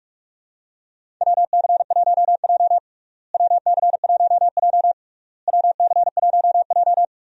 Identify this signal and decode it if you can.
Morse code, heard as a keyed tone, about 46 wpm, 700 Hz. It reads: WC1J WC1J WC1J
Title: USA Call Signs - 4 Characters - Repeated 3x - 45wpm